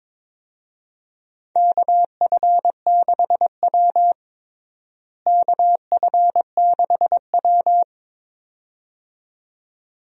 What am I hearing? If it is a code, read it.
Morse code, heard as a keyed tone, about 22 wpm, 700 Hz. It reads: KF6W KF6W
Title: USA Call Signs - 4 Characters - Repeated 2x - 22wpm